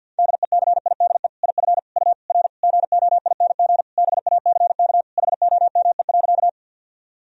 Morse code, heard as a keyed tone, about 51 wpm, 700 Hz. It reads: DECIDE IF U R GOING BACK HOME.
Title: Sentences from Top 500 Words and Abbreviations 50wpm